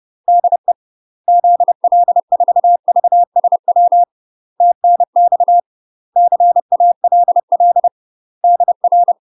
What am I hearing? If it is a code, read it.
Morse code, heard as a keyed tone, about 30 wpm, 700 Hz. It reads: DE ZL4VSW TNX CALL DR
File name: UK MASTER QSOs TRAINING / 30WPM / UK MASTER QSOs 0019